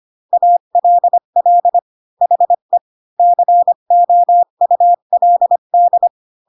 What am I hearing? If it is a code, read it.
Morse code, heard as a keyed tone, about 25 wpm, 700 Hz. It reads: ALL HE COULD
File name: Pinocchio / Pinocchio ch3 20 25 ik1hge